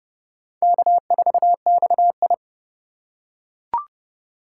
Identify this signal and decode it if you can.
Morse code, heard as a keyed tone, about 28 wpm, 700 Hz. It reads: K4XI